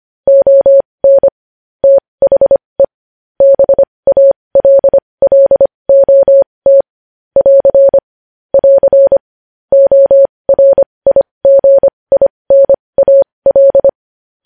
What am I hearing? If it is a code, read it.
Morse code, heard as a keyed tone, about 25 wpm, spaced as for 20 wpm, 550 Hz. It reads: ON THE BALLOT <AR> <AR> ORIGINAL
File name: podcast / News headlines-1x-25x20